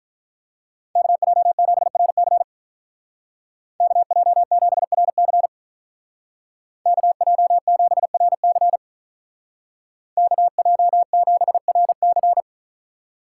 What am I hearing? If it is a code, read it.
Morse code, heard as a keyed tone, about 46 wpm, 700 Hz. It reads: KJ7RC KJ7RC KJ7RC KJ7RC